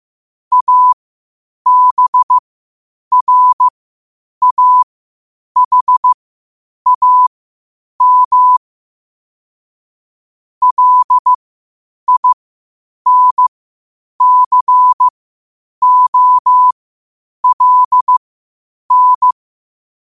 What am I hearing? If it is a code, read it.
Morse code, heard as a keyed tone, about 15 wpm, spaced as for 8 wpm, 1000 Hz. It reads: ABRAHAM LINCOLN
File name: qotdfiles / 8wpm